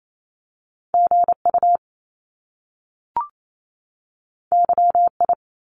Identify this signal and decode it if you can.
Morse code, heard as a keyed tone, about 28 wpm, 700 Hz. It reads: GU YI
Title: Top Two Letter Combinations in Words 28wpm